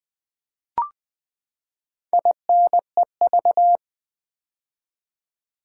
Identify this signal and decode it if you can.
Morse code, heard as a keyed tone, about 20 wpm, 700 Hz. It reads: INEV
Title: Build A Word - 4 Letters At A Time - 10-Letter Words - 20wpm